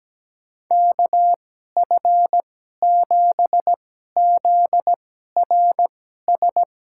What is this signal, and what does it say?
Morse code, heard as a keyed tone, about 17 wpm, spaced as for 12 wpm, 700 Hz. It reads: KF7ZRS
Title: USA Call Signs - 6 Characters - 2x Character Spacing - 17wpm